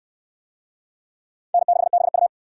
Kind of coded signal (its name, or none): Morse code